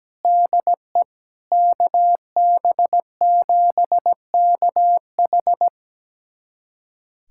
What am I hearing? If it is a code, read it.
Morse code, heard as a keyed tone, about 17 wpm, 700 Hz. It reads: DE KB7KH
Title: USA Callsign DE Callsign 17wpm